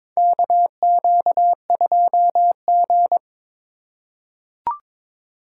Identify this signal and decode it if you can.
Morse code, heard as a keyed tone, about 22 wpm, 700 Hz. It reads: KQ2G
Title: USA Call Signs - 4 Characters - 3x Character Spacing - 22wpm